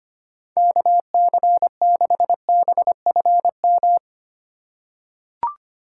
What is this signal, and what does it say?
Morse code, heard as a keyed tone, about 25 wpm, 700 Hz. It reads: KC6BFM
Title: USA Call Signs - 6 Characters - 3x Character Spacing - 25wpm